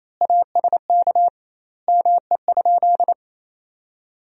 Morse code, heard as a keyed tone, about 28 wpm, 700 Hz. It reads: ASK ME?
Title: 2x Word-Spacing - Sentences from Top 200 Words - 28wpm